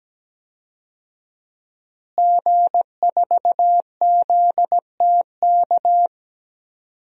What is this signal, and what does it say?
Morse code, heard as a keyed tone, about 17 wpm, 700 Hz. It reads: G4ZTK
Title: DXCC-100 Call Signs - 5 Characters - Repeated 2x - 17wpm